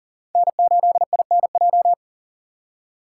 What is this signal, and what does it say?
Morse code, heard as a keyed tone, about 52 wpm, 700 Hz. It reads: N8INJ